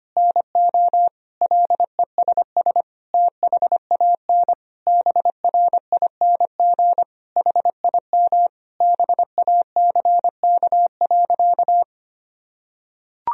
Morse code, heard as a keyed tone, about 25 wpm, 700 Hz. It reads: NO LESS THAN BRING HIM BACK.
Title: Sentences from Top 1000 Words - Review Part 3 - 25wpm